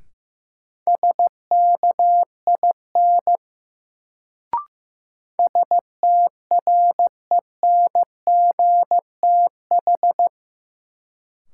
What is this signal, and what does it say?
Morse code, heard as a keyed tone, about 15 wpm, 700 Hz. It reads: SKIN STRENGTH
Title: Sentences from Top 1300 Words - New Word Drill Alt - 15wpm